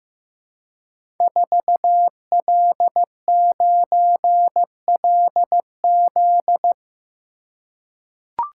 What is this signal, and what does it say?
Morse code, heard as a keyed tone, about 15 wpm, 700 Hz. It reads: 4L9LZ
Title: Code Groups - 5 Letter:Numbers - 15wpm